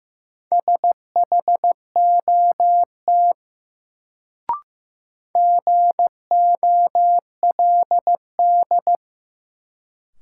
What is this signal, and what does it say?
Morse code, heard as a keyed tone, about 15 wpm, 700 Hz. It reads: SHOT GOLD
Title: Sentences from Top 1100 Words - New Word Drill Alt - 15wpm